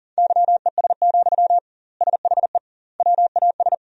Morse code, heard as a keyed tone, about 40 wpm, 700 Hz. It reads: YES, SHE WAS
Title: 2x Word-Spacing - Sentences from Top 500 Words - 40wpm